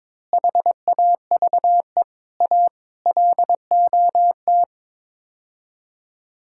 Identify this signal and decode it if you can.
Morse code, heard as a keyed tone, about 22 wpm, 700 Hz. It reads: HAVE A LOT
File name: practice / 22wpm / Build A Phrase - 5-Word Phrases - 22wpm